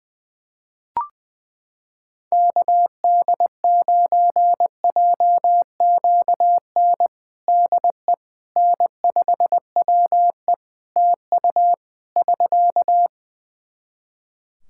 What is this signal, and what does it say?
Morse code, heard as a keyed tone, about 20 wpm, 700 Hz. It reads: KD9JQN DE N5WE TU <SK>
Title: Practice QSO - Exchange 4 - Ending - 20wpm